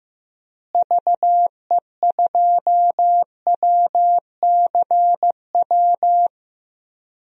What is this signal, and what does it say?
Morse code, heard as a keyed tone, about 15 wpm, 700 Hz. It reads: VE2WCW
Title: DXCC-100 Call Signs - 6 Characters - Repeated 3x - 15wpm